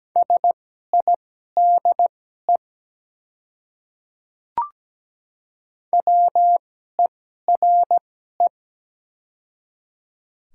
Morse code, heard as a keyed tone, about 17 wpm, spaced as for 12 wpm, 700 Hz. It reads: SIDE WERE
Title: 2x Character Spacing - Top 300 Words - 17wpm